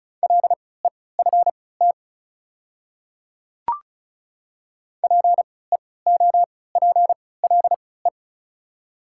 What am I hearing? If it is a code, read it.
Morse code, heard as a keyed tone, about 36 wpm, spaced as for 20 wpm, 700 Hz. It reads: LEFT PEOPLE
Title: 3x Character Spacing - Top 300 Words - 35wpm